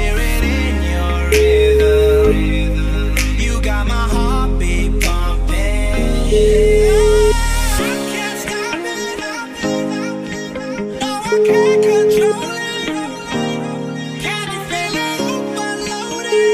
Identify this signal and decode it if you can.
Ringback tone (Europe, 425 Hz), with noise.